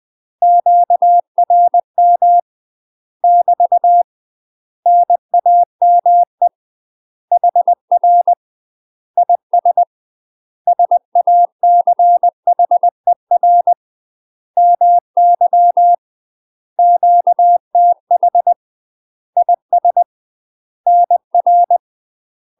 Morse code, heard as a keyed tone, about 20 wpm, 700 Hz. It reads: QRM <BT> NAME HR IS SACHER MY QTH IS NR
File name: UK MASTER QSOs TRAINING / 20WPM / UK MASTER QSOs 0051